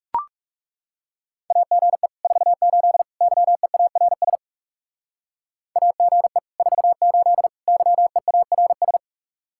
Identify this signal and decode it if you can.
Morse code, heard as a keyed tone, about 40 wpm, 700 Hz. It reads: AGE 48 YEARS AGE 48 YEARS